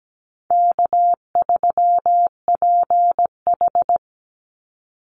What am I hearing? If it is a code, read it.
Morse code, heard as a keyed tone, about 17 wpm, 700 Hz. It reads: K3PH
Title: Easy Call Signs Repeated 3x - 17wpm